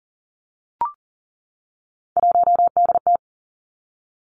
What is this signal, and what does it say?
Morse code, heard as a keyed tone, about 41 wpm, 700 Hz. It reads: 1DT
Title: Three Challenging Letter-Number Combinations 40wpm